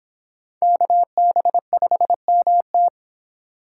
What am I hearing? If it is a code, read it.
Morse code, heard as a keyed tone, about 26 wpm, 700 Hz. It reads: KB5MT